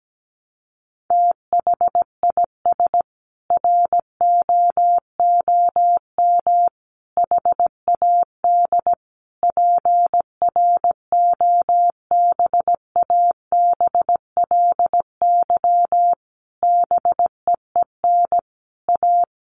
Morse code, heard as a keyed tone, about 17 wpm, 700 Hz. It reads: THIS ROOM HAD PROBABLY BEEN A
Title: Sentences from Top 700 Words 17wpm